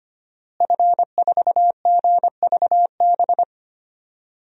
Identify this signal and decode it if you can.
Morse code, heard as a keyed tone, about 25 wpm, 700 Hz. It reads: F4GVB
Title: DXCC-100 Call Signs - 5 Characters - Repeated 3x - 25wpm